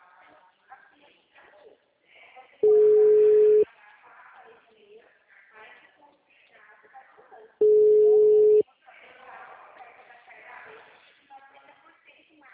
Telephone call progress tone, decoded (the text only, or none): ringback tone